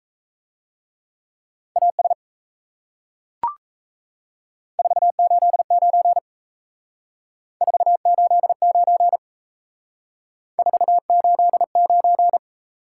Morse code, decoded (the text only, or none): AS 489 489 489